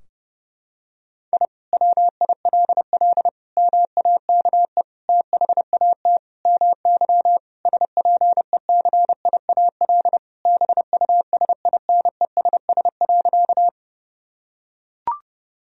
Morse code, heard as a keyed tone, about 32 wpm, 700 Hz. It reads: I WILL MAKE THAT MY SPECIAL BUSINESS.
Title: Sentences from Top 1000 Words - Review Part 3 - 30wpm